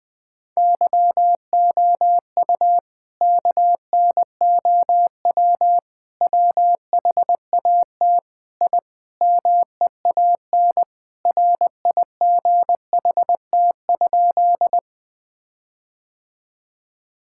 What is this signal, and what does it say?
Morse code, heard as a keyed tone, about 20 wpm, 700 Hz. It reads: YOU KNOW WHAT I MEAN RIGHT?
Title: Sentences with 5-Word Phrases - 20wpm